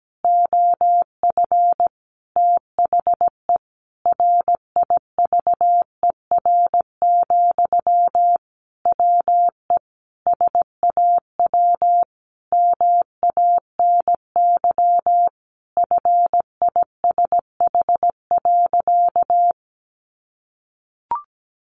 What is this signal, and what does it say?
Morse code, heard as a keyed tone, about 17 wpm, 700 Hz. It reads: OF THE RIVER, WE SAW MANY FISH.
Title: Sentences from Top 1500 Words 17wpm